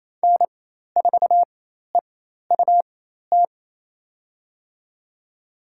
Morse code, heard as a keyed tone, about 28 wpm, spaced as for 13 wpm, 700 Hz. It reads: N4EUT
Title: USA Call Signs - 5 Characters - 4x Character Spacing - 28wpm